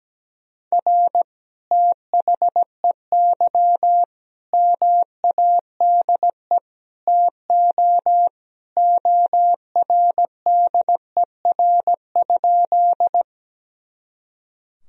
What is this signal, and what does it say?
Morse code, heard as a keyed tone, about 17 wpm, 700 Hz. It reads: R THEY MADE TO ORDER?
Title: Sentences from Top 400 Words and Abbreviations 17wpm